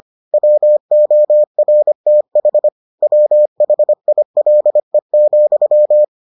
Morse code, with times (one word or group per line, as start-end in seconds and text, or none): 0.3-2.7 WORTH
3.0-6.0 WHILE,